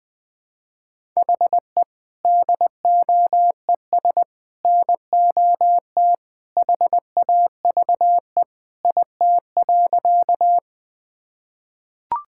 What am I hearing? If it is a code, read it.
Morse code, heard as a keyed tone, about 21 wpm, 700 Hz. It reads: HE DOES NOT HAVE IT.
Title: Sentences with 3-Word Phrases - Spoken Beginning - 20wpm